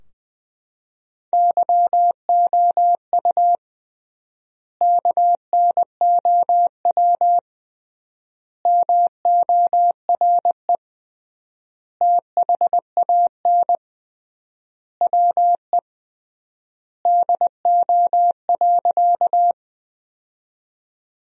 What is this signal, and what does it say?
Morse code, heard as a keyed tone, about 21 wpm, 700 Hz. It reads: YOU KNOW MORE THAN WE DO.
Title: 3x Word-Spacing - Sentences from Top 100 Words - 20wpm